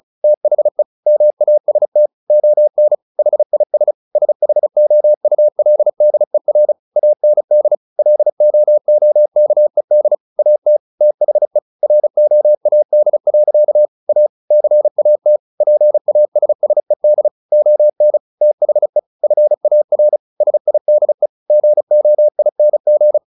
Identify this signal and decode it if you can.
Morse code, heard as a keyed tone, about 34 wpm, 600 Hz. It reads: THE MAST ON HIS SHOULDER AND LOOKED AT THE ROAD. A CAT PASSED ON THE FAR SIDE GOING
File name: the old man and the sea 2 35wpm